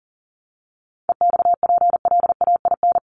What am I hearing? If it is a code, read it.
Morse code, heard as a keyed tone, about 40 wpm, 700 Hz. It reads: EXPLAIN
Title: Sentences from Top 1300 Words 40wpm